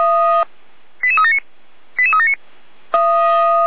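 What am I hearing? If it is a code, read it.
Five-tone selective call (ZVEI), heard as quick runs of five tones, tones of about 70 ms. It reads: 88379 88379
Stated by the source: ZVEI five-tone selective call received by SDR.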